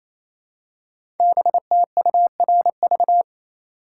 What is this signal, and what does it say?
Morse code, heard as a keyed tone, about 28 wpm, 700 Hz. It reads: BTURV
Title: Code Groups - 5 Letters - 28wpm